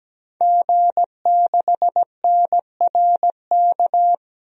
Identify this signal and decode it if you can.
Morse code, heard as a keyed tone, about 17 wpm, 700 Hz. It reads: G6NRK
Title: DXCC-100 Call Signs - 5 Characters - 17wpm